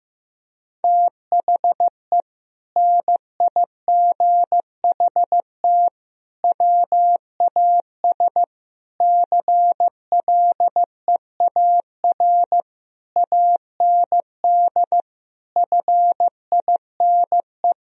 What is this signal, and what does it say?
Morse code, heard as a keyed tone, about 15 wpm, 700 Hz. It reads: THE NIGHT WAS CLEAR AND FINE
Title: Sentences from Top 1000 Words - Review Part 2 - 15wpm